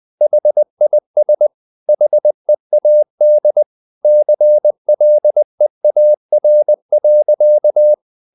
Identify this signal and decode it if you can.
Morse code, heard as a keyed tone, about 20 wpm, 600 Hz. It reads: HIS HEAD CLEAR.
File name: the old man and the sea part 2 20wpm